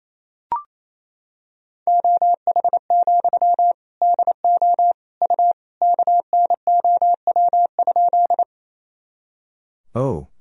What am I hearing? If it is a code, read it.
Morse code, heard as a keyed tone, about 28 wpm, 700 Hz. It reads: OH, DO U KNOW?
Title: Sentences from Top 500 Words and Abbreviations 28wpm